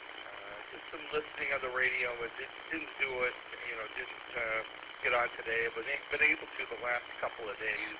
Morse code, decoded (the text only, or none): none